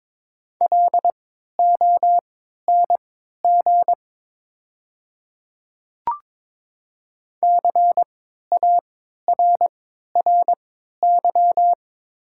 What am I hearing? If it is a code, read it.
Morse code, heard as a keyed tone, about 22 wpm, spaced as for 12 wpm, 700 Hz. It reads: LONG CARRY
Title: 3x Character Spacing - Top 300 Words - 22wpm